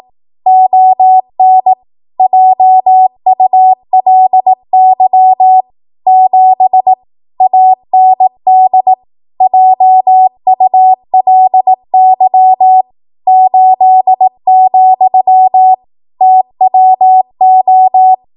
Morse code, heard as a keyed tone, about 18 wpm, 750 Hz. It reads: ON JULY 7 AND JULY 8, TWO
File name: Archive / 18 WPM / 18WPM